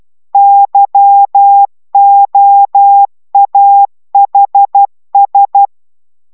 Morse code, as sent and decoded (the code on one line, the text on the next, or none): -.-- --- .- .... ...
YOAHS